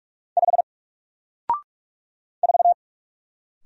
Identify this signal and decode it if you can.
Morse code, heard as a keyed tone, about 47 wpm, 700 Hz. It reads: 5 4
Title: Single Character (Letter:Number:Punctuation:Prosign) Alt 45wpm